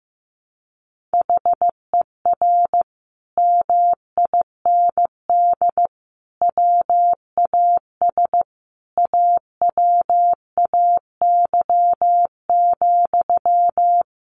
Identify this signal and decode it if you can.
Morse code, heard as a keyed tone, about 15 wpm, 700 Hz. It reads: HER MIND WAS AWAY,
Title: Sentences from Top 1500 Words 15wpm